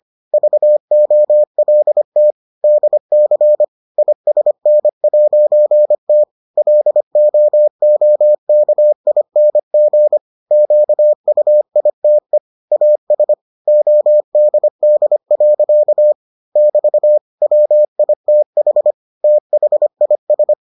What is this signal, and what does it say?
Morse code, heard as a keyed tone, about 25 wpm, 600 Hz. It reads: VOLT DC ISN'T LOOKING QUITE AS ODD. <BT> WITH THIS